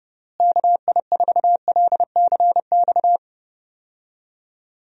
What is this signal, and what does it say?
Morse code, heard as a keyed tone, about 30 wpm, 700 Hz. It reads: KI4LCX
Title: USA Call Signs - 6 Characters - Repeated 2x - 30wpm